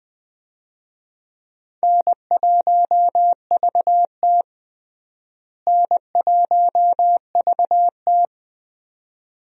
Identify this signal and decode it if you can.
Morse code, heard as a keyed tone, about 20 wpm, 700 Hz. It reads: N1VT N1VT
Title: USA Call Signs - 4 Characters - Repeated 2x - 20wpm